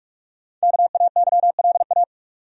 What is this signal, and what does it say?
Morse code, heard as a keyed tone, about 46 wpm, 700 Hz. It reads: KAYLA
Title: USA Common Names - Max 5 Letters - 45wpm